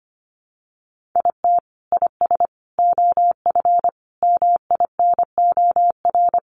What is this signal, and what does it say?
Morse code, heard as a keyed tone, about 25 wpm, 700 Hz. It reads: IT IS OF MINOR
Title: Sentences from Top 1500 Words 25wpm